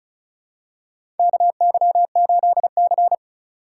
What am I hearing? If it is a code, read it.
Morse code, heard as a keyed tone, about 36 wpm, 700 Hz. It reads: KY8C
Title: USA Call Signs - 4 Characters - 3x Character Spacing - 35wpm